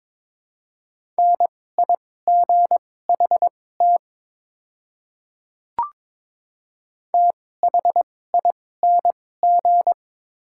Morse code, heard as a keyed tone, about 22 wpm, spaced as for 16 wpm, 700 Hz. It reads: NIGHT THING
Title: 2x Character Spacing - Top 300 Words - 22wpm